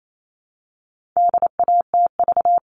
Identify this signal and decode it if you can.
Morse code, heard as a keyed tone, about 28 wpm, 700 Hz. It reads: DATV
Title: All Cut Numbers - 4-Digit 28wpm